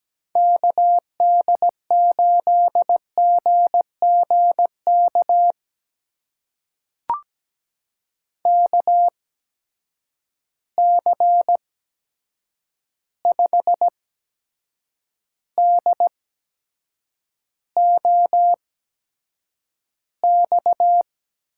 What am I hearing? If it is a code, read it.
Morse code, heard as a keyed tone, about 17 wpm, spaced as for 7 wpm, 700 Hz. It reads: KD8GGK KC5DOX
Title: USA Call Signs - 6 Characters - 8x Character Spacing - 17wpm